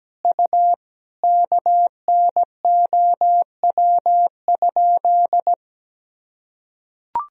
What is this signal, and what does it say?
Morse code, heard as a keyed tone, about 17 wpm, 700 Hz. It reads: U KNOW?
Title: Sentences from Top 100 Words and Abbreviations 17wpm